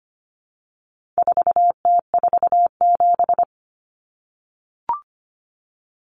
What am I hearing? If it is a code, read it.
Morse code, heard as a keyed tone, about 25 wpm, 700 Hz. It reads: 4T47
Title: Common Cut Numbers - 4-Digit 25wpm